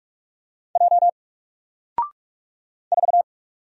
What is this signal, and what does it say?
Morse code, heard as a keyed tone, about 44 wpm, 700 Hz. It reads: J 4
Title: Single Letter-Number 45wpm